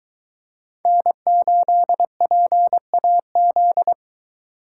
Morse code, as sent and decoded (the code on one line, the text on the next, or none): -. ---.. .--. .- --..
N8PAZ